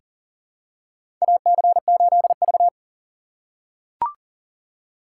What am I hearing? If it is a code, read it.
Morse code, heard as a keyed tone, about 40 wpm, 700 Hz. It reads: AC8V
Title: USA Call Signs - 4 Characters - 2x Character Spacing - 40wpm